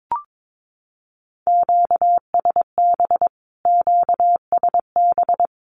Morse code, heard as a keyed tone, about 22 wpm, 700 Hz. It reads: QSB QSB
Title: Q-Codes Repeated 3x 22wpm